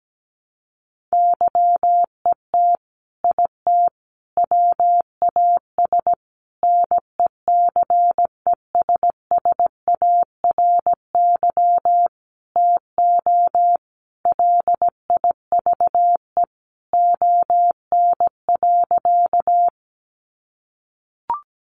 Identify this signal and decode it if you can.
Morse code, heard as a keyed tone, about 17 wpm, 700 Hz. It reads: YET IT WAS NECESSARY TO LIVE ON.
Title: Sentences from Top 800 Words 17wpm